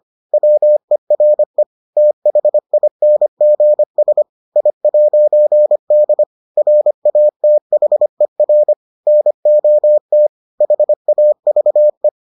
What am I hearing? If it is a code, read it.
Morse code, heard as a keyed tone, about 25 wpm, 600 Hz. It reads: WERE THINGS I'D RATHER NOT HAVE